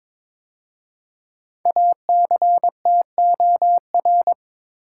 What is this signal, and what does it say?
Morse code, heard as a keyed tone, about 22 wpm, 700 Hz. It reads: ACTOR
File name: 5 Letter Words - First 3 Letters Given - 22wpm